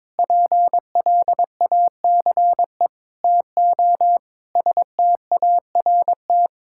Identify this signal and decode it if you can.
Morse code, heard as a keyed tone, about 22 wpm, 700 Hz. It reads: PLACE TO START
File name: practice / 22wpm / Sets of 6 Words - 22wpm